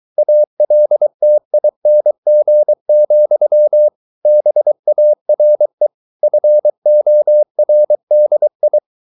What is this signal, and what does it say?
Morse code, heard as a keyed tone, about 23 wpm, 600 Hz. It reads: ALTING, BARE FORDI
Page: Ferie 23 wpm